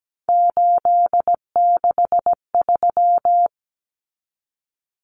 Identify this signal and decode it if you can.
Morse code, heard as a keyed tone, about 17 wpm, 700 Hz. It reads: 863
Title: Common Cut Numbers - 3-Digit 17wpm